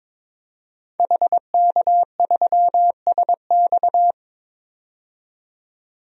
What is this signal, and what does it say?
Morse code, heard as a keyed tone, about 22 wpm, 700 Hz. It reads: HK3SX
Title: DXCC-100 Call Signs - 5 Characters - 22wpm